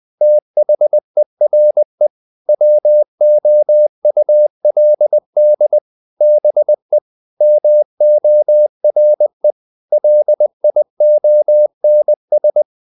Morse code, heard as a keyed tone, about 20 wpm, 600 Hz. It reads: THERE WOULD BE MORE LIONS
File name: the old man and the sea part 2 20wpm